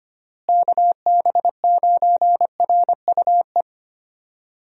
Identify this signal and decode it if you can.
Morse code, heard as a keyed tone, about 25 wpm, 700 Hz. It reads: KB9RUE
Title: USA Call Signs - 6 Characters - Repeated 3x - 25wpm